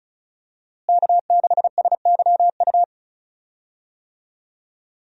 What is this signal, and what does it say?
Morse code, heard as a keyed tone, about 34 wpm, 700 Hz. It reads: K6SYU
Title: USA Call Signs - 5 Characters - Repeated 2x - 35wpm